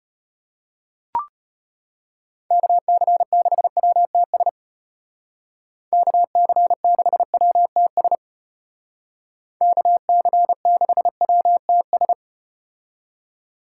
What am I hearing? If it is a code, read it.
Morse code, heard as a keyed tone, about 34 wpm, 700 Hz. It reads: KC6WTS KC6WTS KC6WTS